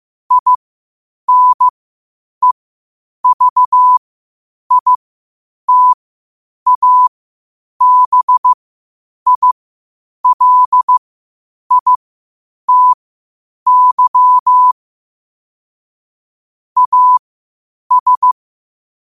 Morse code, heard as a keyed tone, about 15 wpm, spaced as for 8 wpm, 1000 Hz. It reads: INEVITABILITY AS
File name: qotdfiles / 8wpm